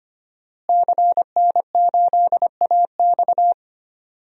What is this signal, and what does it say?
Morse code, heard as a keyed tone, about 25 wpm, 700 Hz. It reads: CN8AX
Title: DXCC-100 Call Signs - 5 Characters - Repeated 2x - 25wpm